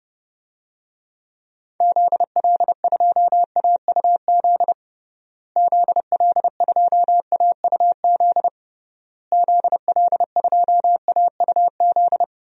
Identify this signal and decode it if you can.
Morse code, heard as a keyed tone, about 30 wpm, 700 Hz. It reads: ZL2AUZ ZL2AUZ ZL2AUZ
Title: DXCC-100 Call Signs - 6 Characters - Repeated 3x - 30wpm